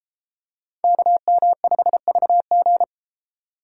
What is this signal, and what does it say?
Morse code, heard as a keyed tone, about 31 wpm, 700 Hz. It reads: KM5VG